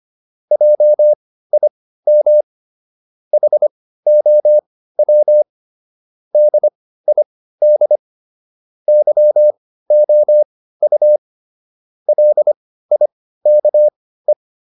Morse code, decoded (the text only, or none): JIM HOW DID YOU LIKE